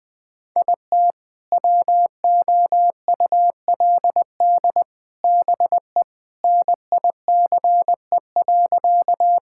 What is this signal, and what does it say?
Morse code, heard as a keyed tone, about 20 wpm, 700 Hz. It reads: IT WOULD BE NICE.
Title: Sentences with 3-Word Phrases - 20wpm